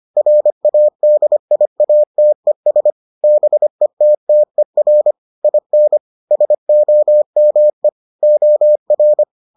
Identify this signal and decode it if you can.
Morse code, heard as a keyed tone, about 25 wpm, 600 Hz. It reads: RADIATES BETTER IN SOME OR